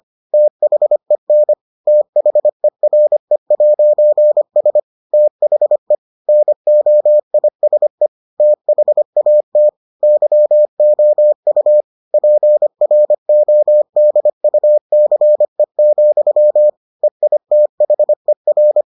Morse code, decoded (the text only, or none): THEN THERE'S THE NOISE THAT YOU PRODUCE, EITHER